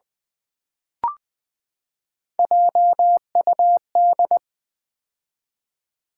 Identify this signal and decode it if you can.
Morse code, heard as a keyed tone, about 20 wpm, 700 Hz. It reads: JUD
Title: Build A Word - 3 Letters At A Time - 9-Letter Words - 20wpm